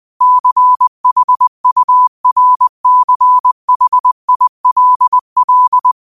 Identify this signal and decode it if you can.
Morse code, heard as a keyed tone, about 20 wpm, 1000 Hz. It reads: CHURCHILL